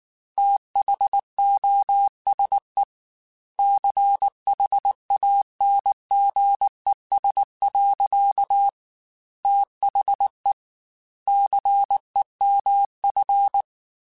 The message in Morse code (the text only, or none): THOSE CHANGES. THE CEMF